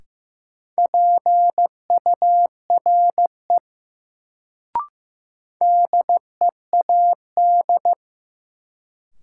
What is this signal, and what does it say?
Morse code, heard as a keyed tone, about 15 wpm, 700 Hz. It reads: PURE DEAD
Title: Sentences from Top 1500 Words - New Word Drill - 15wpm